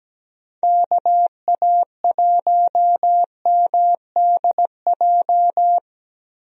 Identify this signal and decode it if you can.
Morse code, heard as a keyed tone, about 17 wpm, 700 Hz. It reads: KA1MDJ
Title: USA Call Signs - 6 Characters - Repeated 3x - 17wpm